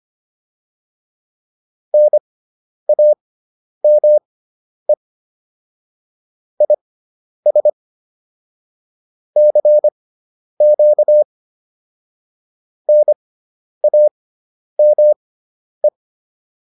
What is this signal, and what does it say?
Morse code, heard as a keyed tone, about 25 wpm, spaced as for 10 wpm, 600 Hz. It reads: NAME IS CQ NAME
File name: QSO108 10